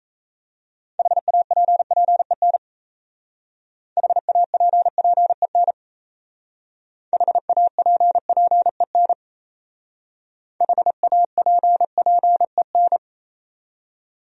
Morse code, heard as a keyed tone, about 36 wpm, 700 Hz. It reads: HAPPEN HAPPEN HAPPEN HAPPEN